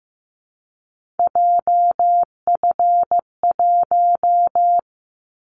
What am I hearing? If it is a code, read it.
Morse code, heard as a keyed tone, about 15 wpm, 700 Hz. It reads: JF1
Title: Callsign Prefixes for Top 25 DXCC - Repeated 3x - 15wpm